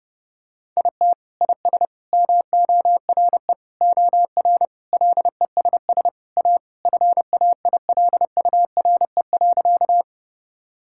Morse code, heard as a keyed tone, about 32 wpm, 700 Hz. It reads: IT IS MORE OR LESS A FAILURE.
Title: Sentences from Top 1000 Words - Review Part 6 - 30wpm